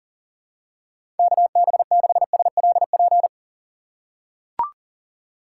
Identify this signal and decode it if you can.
Morse code, heard as a keyed tone, about 40 wpm, 700 Hz. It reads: KB6SLP
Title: USA Call Signs - 6x Character Spacing - 40wpm